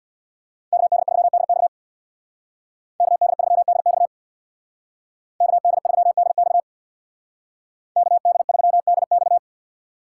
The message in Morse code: KD3DX KD3DX KD3DX KD3DX

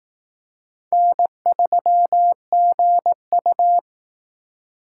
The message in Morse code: N3GU

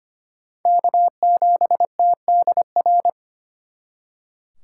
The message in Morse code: K7TDR

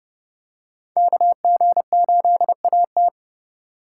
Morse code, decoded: KG8AT

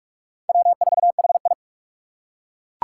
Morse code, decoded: W4HI

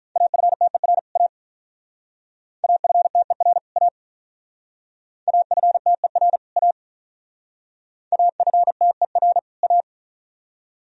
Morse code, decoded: AFTER A AFTER A AFTER A AFTER A

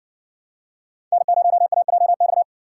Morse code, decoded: N0AJX

